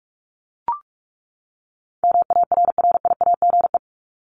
Morse code, heard as a keyed tone, about 44 wpm, 700 Hz. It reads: MARRIAGE
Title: Top 751-1000 Words 45wpm